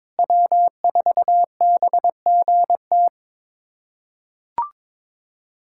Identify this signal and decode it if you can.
Morse code, heard as a keyed tone, about 21 wpm, 700 Hz. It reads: W4BGT
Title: USA Callsign DE Callsign - 5 Characters - 22wpm